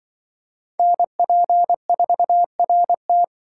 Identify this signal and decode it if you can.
Morse code, heard as a keyed tone, about 25 wpm, 700 Hz. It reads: NP4RT
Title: Speed Racing - USA Call Signs 20wpm